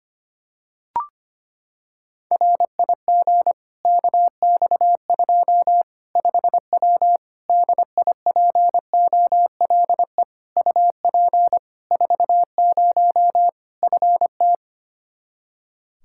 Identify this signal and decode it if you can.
Morse code, heard as a keyed tone, about 25 wpm, 700 Hz. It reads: RIG KX2 5W DIPOLE UP 40 FT
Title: Practice QSO - Exchange 2 - Info - 25wpm